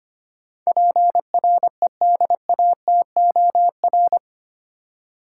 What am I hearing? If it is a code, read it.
Morse code, heard as a keyed tone, about 25 wpm, 700 Hz. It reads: PREDATOR
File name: 8 Letter Words - First 3 Letters Given - 25wpm